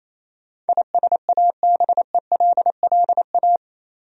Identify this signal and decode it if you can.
Morse code, heard as a keyed tone, about 27 wpm, 700 Hz. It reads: ISABELLA
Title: USA Common Names - 28wpm